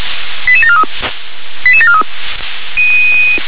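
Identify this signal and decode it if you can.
Five-tone selective call (ZVEI), heard as quick runs of five tones, tones of about 70 ms. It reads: 88743 88743
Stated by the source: ZVEI five-tone selective call received on an SDR.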